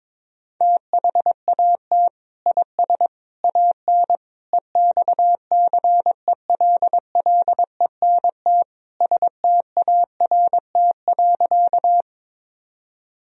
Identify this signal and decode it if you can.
Morse code, heard as a keyed tone, about 22 wpm, 700 Hz. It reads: THAT IS AN EXCELLENT START.
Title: Sentences from Top 1000 Words - Review Part 1 - 22wpm